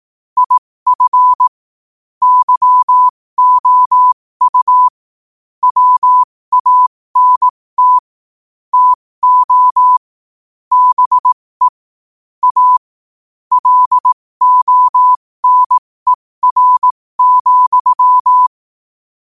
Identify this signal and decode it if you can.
Morse code, heard as a keyed tone, about 18 wpm, spaced as for 15 wpm, 1000 Hz. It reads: IF YOU WANT TO BE A LONER,
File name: qotdfiles / 15wpm